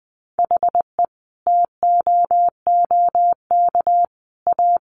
Morse code, heard as a keyed tone, about 19 wpm, 700 Hz. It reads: HE TOOK A
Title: Sentences from Top 1000 Words 20wpm